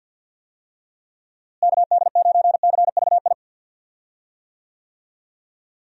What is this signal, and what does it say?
Morse code, heard as a keyed tone, about 52 wpm, 700 Hz. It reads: KD9CVI
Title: USA Call Signs - 6 Characters - Repeated 3x - 50wpm